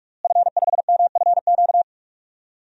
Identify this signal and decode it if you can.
Morse code, heard as a keyed tone, about 42 wpm, 700 Hz. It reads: F5MFQ
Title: DXCC-100 Call Signs - 5 Characters - Repeated 2x - 45wpm